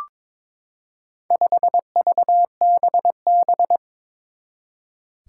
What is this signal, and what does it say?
Morse code, heard as a keyed tone, about 22 wpm, 700 Hz. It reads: 5VBB
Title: Four Challenging Letter-Number Combinations 22wpm